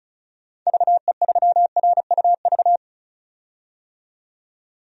Morse code, heard as a keyed tone, about 36 wpm, 700 Hz. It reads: VE3RUV
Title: DXCC-100 Call Signs - 6 Characters - 35wpm